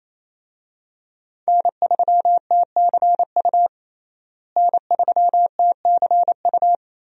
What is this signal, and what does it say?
Morse code, heard as a keyed tone, about 27 wpm, 700 Hz. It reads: N3TCU N3TCU
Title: USA Call Signs - 5 Characters - Repeated 2x - 28wpm